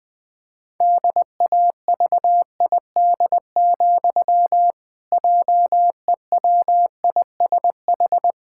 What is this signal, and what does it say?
Morse code, heard as a keyed tone, about 20 wpm, 700 Hz. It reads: DAVID, JEWISH
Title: Pangram Challenge v2 - 20wpm